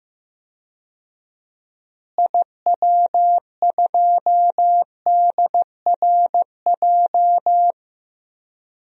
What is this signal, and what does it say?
Morse code, heard as a keyed tone, about 15 wpm, 700 Hz. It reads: IW2DRJ
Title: DXCC-100 Call Signs - 6 Characters - Repeated 3x - 15wpm